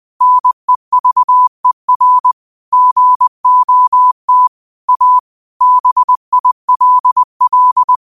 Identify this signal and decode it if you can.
Morse code, heard as a keyed tone, about 20 wpm, 1000 Hz. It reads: NEVER GOT A BILL
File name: qotdfiles / 20wpm